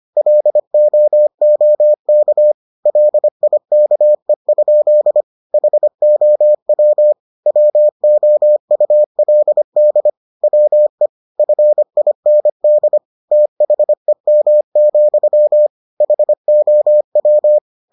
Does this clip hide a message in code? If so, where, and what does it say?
Yes: Morse code from 0.2 to 17.6 s, heard as a keyed tone, about 25 wpm, 600 Hz. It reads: LOOK LIKE? HOW WOULD WE FIND THEM, HOW